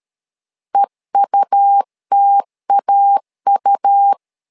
Morse code, heard as a keyed tone, about 13 wpm, 800 Hz. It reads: EUTAU
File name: LEZIONI / 2 UHDN / 2 UH 12wpm 1439